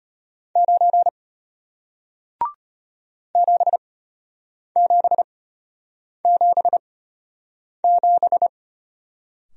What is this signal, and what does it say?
Morse code, heard as a keyed tone, about 33 wpm, spaced as for 8 wpm, 690 Hz. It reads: 9 7777